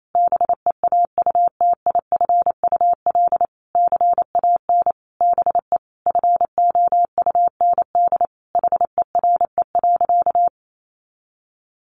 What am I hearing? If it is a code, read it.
Morse code, heard as a keyed tone, about 28 wpm, 700 Hz. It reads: BEAUTIFUL CAN BE FOUND HERE.
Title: Sentences from Top 1500 Words 28wpm